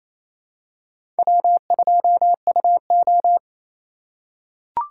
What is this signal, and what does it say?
Morse code, heard as a keyed tone, about 27 wpm, 690 Hz. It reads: W2UO
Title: USA Call Signs - 4 Characters - 2x Character Spacing - 28wpm